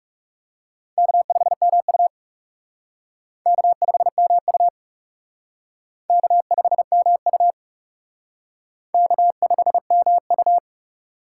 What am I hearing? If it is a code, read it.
Morse code, heard as a keyed tone, about 34 wpm, 700 Hz. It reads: K5MU K5MU K5MU K5MU
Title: Speed Racing - USA 1x2 Call Signs 30wpm